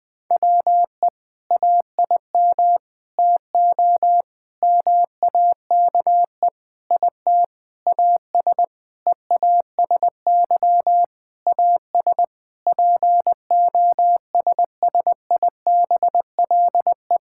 Morse code, heard as a keyed tone, about 20 wpm, 700 Hz. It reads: WE AIM TO MAKE IT AS EASY AS POSSIBLE
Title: Sentences with 7-Word Phrases - Spoken Beginning - 20wpm